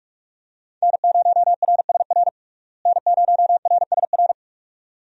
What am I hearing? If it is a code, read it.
Morse code, heard as a keyed tone, about 47 wpm, 700 Hz. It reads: N0RSR N0RSR
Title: USA Call Signs - 5 Characters - Repeated 2x - 45wpm